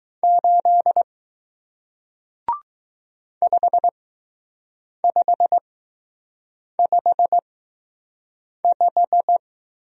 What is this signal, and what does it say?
Morse code, heard as a keyed tone, about 20 wpm, spaced as for 6 wpm, 700 Hz. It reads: 8 5555